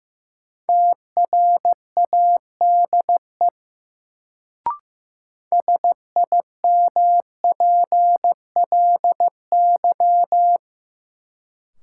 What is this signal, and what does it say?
Morse code, heard as a keyed tone, about 15 wpm, 700 Hz. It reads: TRADE SIMPLY
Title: Sentences from Top 600 Words - New Word Drill - 15wpm